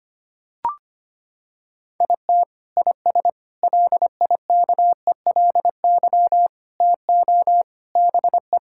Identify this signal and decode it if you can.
Morse code, heard as a keyed tone, about 25 wpm, 700 Hz. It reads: IT IS LIKELY TO BE
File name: practice / 25wpm / Sentences with 3-Word Phrases - 25wpm